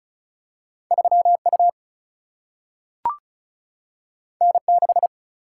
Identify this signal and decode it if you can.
Morse code, heard as a keyed tone, about 35 wpm, 700 Hz. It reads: 3U N6
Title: Single-Letter Single-Number 35wpm